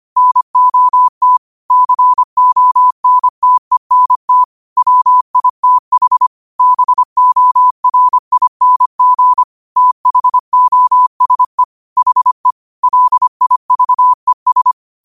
Morse code, heard as a keyed tone, about 25 wpm, 1000 Hz. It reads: NOT CONTENT WITH BORING THOSE HE LIVES